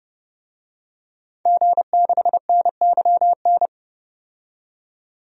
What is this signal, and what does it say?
Morse code, heard as a keyed tone, about 31 wpm, 700 Hz. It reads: G6NYN
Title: DXCC-100 Call Signs - 5 Characters - 30wpm